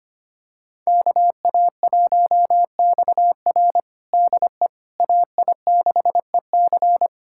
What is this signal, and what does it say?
Morse code, heard as a keyed tone, about 25 wpm, 700 Hz. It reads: KA1XR DE AI6EC
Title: USA Callsign DE Callsign - 5 Characters - 25wpm